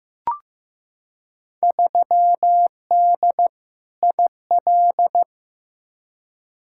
Morse code, heard as a keyed tone, about 15 wpm, 700 Hz. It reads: 3D IL
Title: Field Day Exchanges Encore - 15wpm